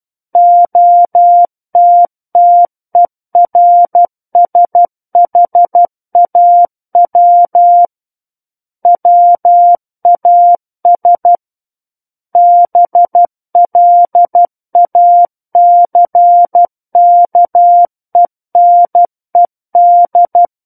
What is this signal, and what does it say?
Morse code, heard as a keyed tone, about 12 wpm, 700 Hz. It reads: OTTERSHAW WAS BLACKENED
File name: Book One / 03-Chapter 3 - 12 WPM